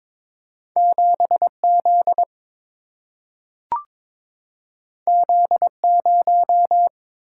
Morse code, heard as a keyed tone, about 22 wpm, 700 Hz. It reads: 7 Z Z 0
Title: Single-Letter Single-Number 22wpm